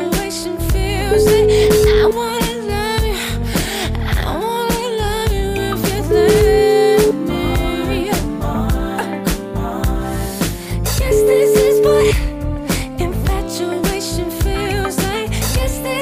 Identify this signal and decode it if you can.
Ringback tone (Europe, 425 Hz), with noise.